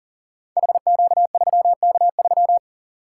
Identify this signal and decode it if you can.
Morse code, heard as a keyed tone, about 42 wpm, 700 Hz. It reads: HQ3K3
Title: Code Groups - 5 Letter:Numbers - 40wpm